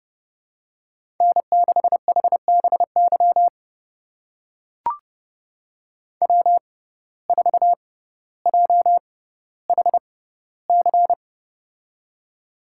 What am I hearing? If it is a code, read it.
Morse code, heard as a keyed tone, about 30 wpm, spaced as for 15 wpm, 700 Hz. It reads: N6HBY W4JHC
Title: USA Call Signs - 5 Characters - 6x Character Spacing - 30wpm